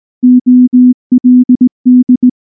Morse code, heard as a keyed tone, about 18 wpm, 250 Hz. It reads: OLD
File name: morse / morse-training-20WPM-8